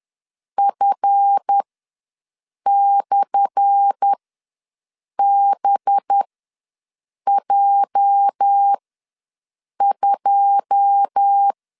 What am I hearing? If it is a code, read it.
Morse code, heard as a keyed tone, about 10 wpm, spaced as for 6 wpm, 800 Hz. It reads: F/BJ2
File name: LEZIONI / 10 SIGNS / 10A SLASH ws 10wpm 4121